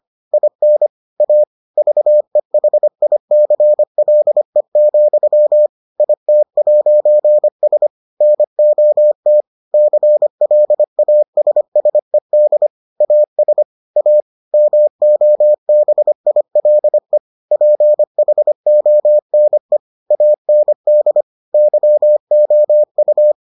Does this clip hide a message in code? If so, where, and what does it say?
Yes: Morse code from 0.3 to 23.3 s, heard as a keyed tone, about 25 wpm, 600 Hz. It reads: IN A VEHICLE, IT'S NOT CLASSED AS A MOBILE PHONE AND YOU